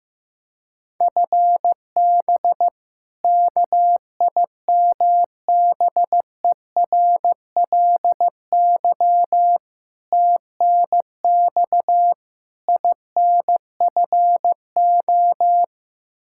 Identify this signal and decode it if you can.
Morse code, heard as a keyed tone, about 15 wpm, 700 Hz. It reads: FB KIMBERLY TNX INFO
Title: Practice QSO - Exchange 3 - More Info - 15wpm